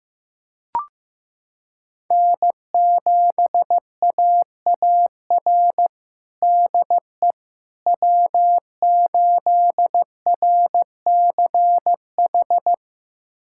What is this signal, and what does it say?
Morse code, heard as a keyed tone, about 15 wpm, 700 Hz. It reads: N7AAR DE W8RCH
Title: USA Callsign DE Callsign - 5 Characters - 15wpm